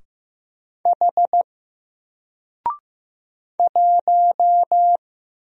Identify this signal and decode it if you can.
Morse code, heard as a keyed tone, about 15 wpm, 700 Hz. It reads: H 1
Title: Single Character (Letter:Number:Punctuation:Prosign) 15wpm